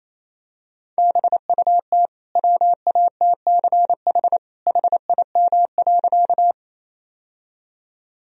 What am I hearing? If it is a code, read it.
Morse code, heard as a keyed tone, about 28 wpm, 700 Hz. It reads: BUT WATCH HIM.
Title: Sentences from Top 1000 Words - Review Part 1 - 28wpm